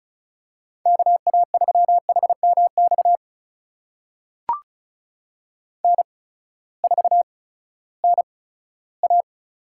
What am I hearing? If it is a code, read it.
Morse code, heard as a keyed tone, about 36 wpm, spaced as for 18 wpm, 700 Hz. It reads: KA3HMX N4NA
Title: USA Call Signs - 8x Character Spacing - 35wpm